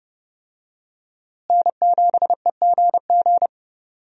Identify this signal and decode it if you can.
Morse code, heard as a keyed tone, about 30 wpm, 700 Hz. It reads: N7EGG
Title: USA Call Signs - 5 Characters - Repeated 3x - 30wpm